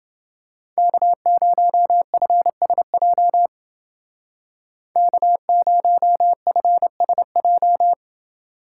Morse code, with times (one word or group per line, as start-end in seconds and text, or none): 0.8-3.5 K0FSJ
5.0-7.9 K0FSJ